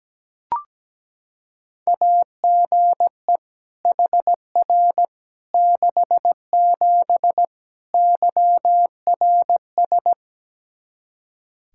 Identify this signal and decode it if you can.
Morse code, heard as a keyed tone, about 17 wpm, 700 Hz. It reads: AGE HR 67 YRS
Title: K1USN SST QSO Practice 17wpm